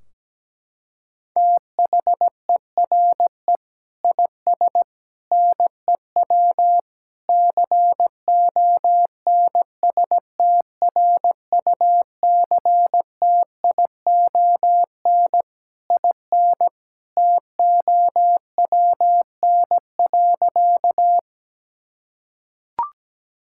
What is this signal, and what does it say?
Morse code, heard as a keyed tone, about 17 wpm, 700 Hz. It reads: THERE IS NEW CONSTRUCTION IN TOWN.
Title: Sentences from Top 1600 Words 17wpm